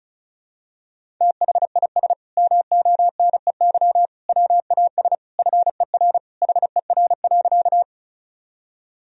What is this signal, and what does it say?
Morse code, heard as a keyed tone, about 36 wpm, 700 Hz. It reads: THIS MONEY WAS FER HER.
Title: Sentences from Top 400 Words and Abbreviations 35wpm